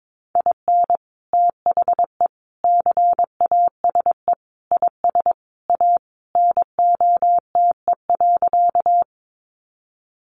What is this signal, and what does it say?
Morse code, heard as a keyed tone, about 22 wpm, 700 Hz. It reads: IN THE CASE IS A NOTE.
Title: Sentences from Top 700 Words 22wpm